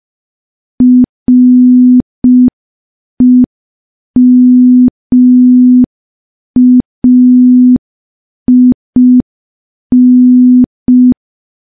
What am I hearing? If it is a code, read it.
Morse code, heard as a keyed tone, about 5 wpm, 250 Hz. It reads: REMAIN
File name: morse / morse-training-05WPM